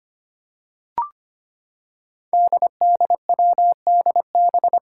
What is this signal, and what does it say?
Morse code, heard as a keyed tone, about 25 wpm, 700 Hz. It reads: DDWDB
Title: Five Challenging Letter-Number Combinations 25wpm